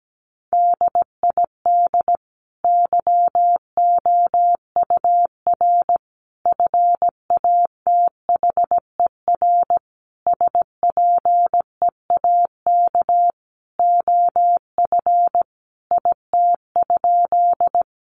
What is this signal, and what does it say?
Morse code, heard as a keyed tone, about 17 wpm, 700 Hz. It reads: DID YOUR FATHER SPEAK OF IT?
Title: Sentences from Top 1100 Words 17wpm